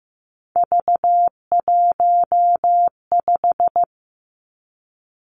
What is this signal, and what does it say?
Morse code, heard as a keyed tone, about 15 wpm, 700 Hz. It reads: V15
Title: Three Challenging Letter-Number Combinations 15wpm